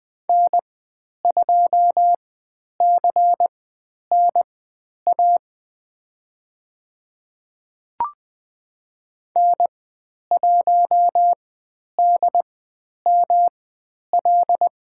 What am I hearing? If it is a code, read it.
Morse code, heard as a keyed tone, about 20 wpm, spaced as for 10 wpm, 700 Hz. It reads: N2CNA N1DML
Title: 109 Lesson 18g - Letter m - Call Signs 20wpm